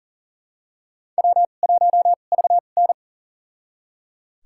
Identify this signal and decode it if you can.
Morse code, heard as a keyed tone, about 40 wpm, spaced as for 29 wpm, 700 Hz. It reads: W1VN
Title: USA Call Signs - 4 Characters - 2x Character Spacing - 40wpm